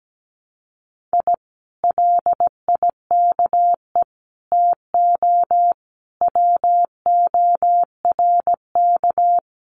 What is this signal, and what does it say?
Morse code, heard as a keyed tone, about 17 wpm, 700 Hz. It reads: I LIKE TO WORK
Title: Common QSO Phrases Encore 17wpm